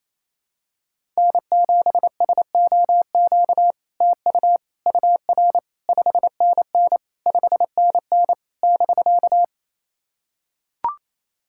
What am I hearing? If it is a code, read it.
Morse code, heard as a keyed tone, about 28 wpm, 700 Hz. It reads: N7SOQ TU UR 5NN 5NN <BK>
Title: SOTA QSO Practice 28wpm